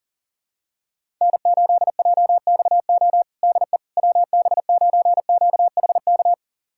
Morse code, heard as a keyed tone, about 40 wpm, 700 Hz. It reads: N8JXO DE WB9QHK
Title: USA Callsign DE Callsign 40wpm